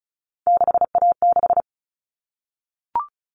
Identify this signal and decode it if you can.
Morse code, heard as a keyed tone, about 34 wpm, 700 Hz. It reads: 6A6
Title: All Cut Numbers - 3-Digit 35wpm